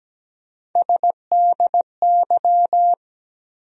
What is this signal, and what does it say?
Morse code, heard as a keyed tone, about 17 wpm, 700 Hz. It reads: SDY
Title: USA Call Sign Postfixes - 3 Letters 17wpm